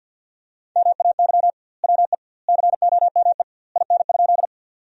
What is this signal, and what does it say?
Morse code, heard as a keyed tone, about 50 wpm, 700 Hz. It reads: MAY WE COME IN?
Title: 2x Word-Spacing - Sentences from Top 100 Words - 50wpm